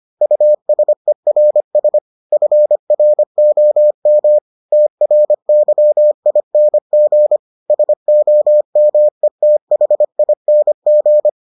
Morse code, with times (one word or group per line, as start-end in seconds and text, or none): 0.2-2.0 USERS
2.3-4.4 FROM
4.7-7.4 TRYING
7.7-11.3 SOMETHING